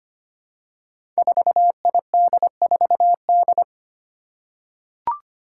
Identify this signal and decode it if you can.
Morse code, heard as a keyed tone, about 25 wpm, 700 Hz. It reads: 4ID4D
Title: Five Challenging Letter-Number Combinations 25wpm